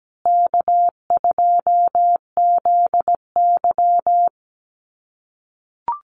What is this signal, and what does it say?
Morse code, heard as a keyed tone, about 17 wpm, 700 Hz. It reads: K2ZY
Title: USA Call Signs Repeated 2x 17wpm